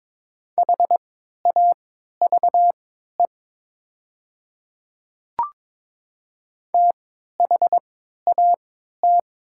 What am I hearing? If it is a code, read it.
Morse code, heard as a keyed tone, about 22 wpm, spaced as for 12 wpm, 700 Hz. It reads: HAVE THAT
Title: 3x Character Spacing - Top 100 Words - 22wpm